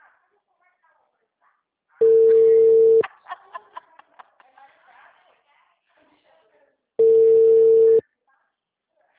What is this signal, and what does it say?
Ringback tone (China, 450 Hz), with speech.
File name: out- -2008- - - .93940